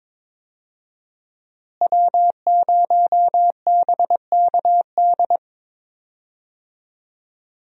Morse code, heard as a keyed tone, about 22 wpm, 700 Hz. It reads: W0BKD
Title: USA Call Signs - 5 Characters - Repeated 2x - 22wpm